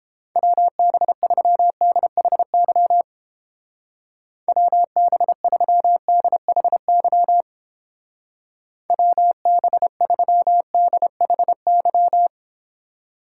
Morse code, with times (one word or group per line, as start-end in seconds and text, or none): 0.4-3.0 WB3DHY
4.5-7.4 WB3DHY
8.9-12.3 WB3DHY